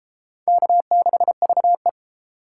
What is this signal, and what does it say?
Morse code, heard as a keyed tone, about 32 wpm, 700 Hz. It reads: K6VE